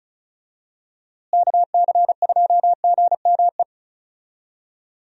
Morse code, heard as a keyed tone, about 36 wpm, 700 Hz. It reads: KC2GME
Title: USA Call Signs - 6 Characters - Repeated 2x - 35wpm